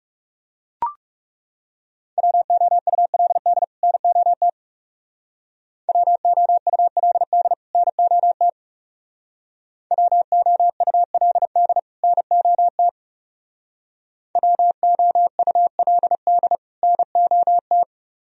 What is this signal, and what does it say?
Morse code, heard as a keyed tone, about 36 wpm, 700 Hz. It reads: WOULD NOT WOULD NOT WOULD NOT WOULD NOT
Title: Speed-Racing - Sets of 2 Words 30wpm